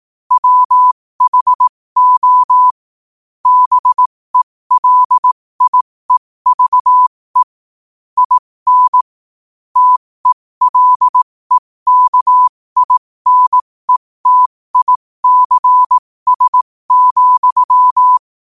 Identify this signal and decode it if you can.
Morse code, heard as a keyed tone, about 18 wpm, spaced as for 15 wpm, 1000 Hz. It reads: WHO BELIEVE IN TELEKINETICS,
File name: qotdfiles / 15wpm